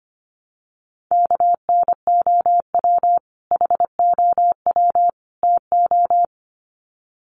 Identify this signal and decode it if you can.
Morse code, heard as a keyed tone, about 25 wpm, 700 Hz. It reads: KNOW HOW TO
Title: Sets of 3 Words 25wpm